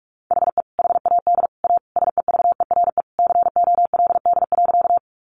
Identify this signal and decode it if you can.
Morse code, heard as a keyed tone, about 44 wpm, 700 Hz. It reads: HE HAD A SEVERE COLD.
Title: Sentences from Top 1400 Words 45wpm